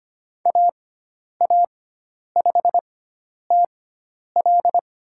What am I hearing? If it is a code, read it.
Morse code, heard as a keyed tone, about 25 wpm, spaced as for 10 wpm, 700 Hz. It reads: AA5TL
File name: Lesson 16g - Letter c - Call Signs 25wpm